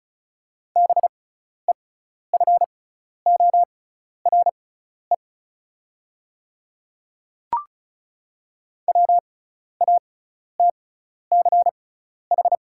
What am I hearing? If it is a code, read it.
Morse code, heard as a keyed tone, about 36 wpm, spaced as for 12 wpm, 690 Hz. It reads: BEFORE WATCH
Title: 6x Character Spacing - Top 300 Words - 35wpm